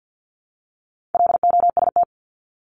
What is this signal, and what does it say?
Morse code, heard as a keyed tone, about 50 wpm, 700 Hz. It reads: LOST
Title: Top 1251-1500 Words 50wpm